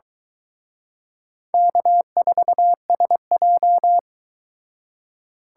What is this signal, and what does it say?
Morse code, heard as a keyed tone, about 23 wpm, 700 Hz. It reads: K4SJ